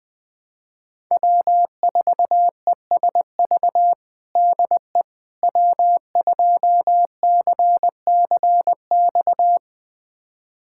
Morse code, heard as a keyed tone, about 20 wpm, 700 Hz. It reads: W4ESV DE W2CCX
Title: USA Callsign DE Callsign - 5 Characters - 20wpm